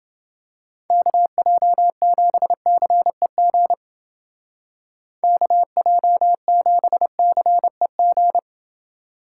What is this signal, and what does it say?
Morse code, heard as a keyed tone, about 29 wpm, 700 Hz. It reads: KJ7CEG KJ7CEG